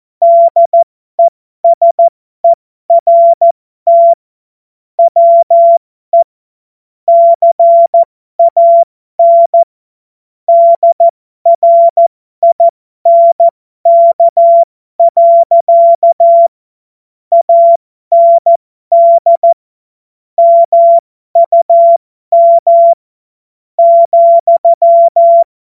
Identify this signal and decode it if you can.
Morse code, heard as a keyed tone, about 14 wpm, spaced as for 12 wpm, 680 Hz. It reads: DESERT WE CAN DRINK. AND MUM ,
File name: Morsekurs / Witze